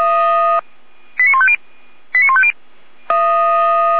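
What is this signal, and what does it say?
Five-tone selective call (ZVEI), heard as quick runs of five tones, tones of about 70 ms. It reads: 89150 89150
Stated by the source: ZVEI five-tone selective call 89078 received on an SDR.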